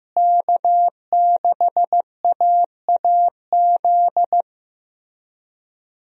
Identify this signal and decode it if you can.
Morse code, heard as a keyed tone, about 14 wpm, 700 Hz. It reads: K6AAZ
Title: USA Call Signs - 5 Characters - Repeated 3x - 15wpm